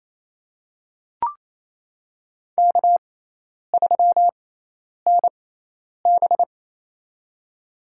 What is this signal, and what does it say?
Morse code, heard as a keyed tone, about 28 wpm, spaced as for 10 wpm, 700 Hz. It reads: K3NB
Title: USA Call Signs - 4 Characters - 6x Character Spacing - 28wpm